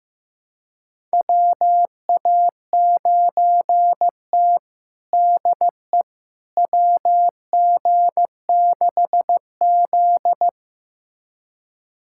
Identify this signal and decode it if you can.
Morse code, heard as a keyed tone, about 15 wpm, 700 Hz. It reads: WA9T DE WG6Z
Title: USA Callsign DE Callsign - 4 Characters - 15wpm